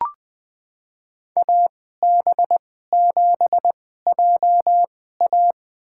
Morse code, heard as a keyed tone, about 20 wpm, spaced as for 14 wpm, 700 Hz. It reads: AB7JA
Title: USA Call Signs - 5 Characters - 2x Character Spacing - 20wpm